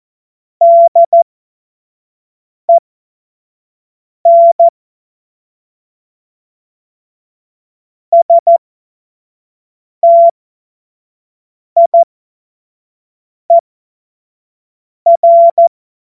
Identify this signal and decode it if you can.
Morse code, heard as a keyed tone, about 14 wpm, spaced as for 5 wpm, 680 Hz. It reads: DEN STIER
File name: Witz-23-September-2025-25-F70